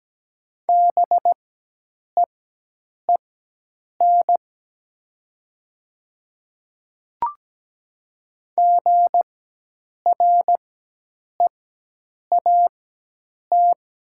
Morse code, heard as a keyed tone, about 17 wpm, spaced as for 8 wpm, 700 Hz. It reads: BEEN GREAT
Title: 4x Character Spacing - Top 200 Words - 17wpm